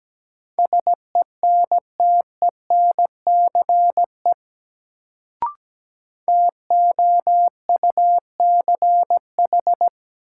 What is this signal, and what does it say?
Morse code, heard as a keyed tone, about 17 wpm, 700 Hz. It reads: SENTENCE TOUCH
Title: Sentences from Top 1200 Words - New Word Drill Alt - 17wpm